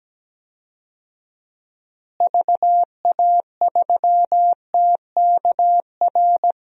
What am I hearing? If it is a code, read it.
Morse code, heard as a keyed tone, about 17 wpm, 700 Hz. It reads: VA3TKR
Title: DXCC-100 Call Signs - 6 Characters - Repeated 2x - 17wpm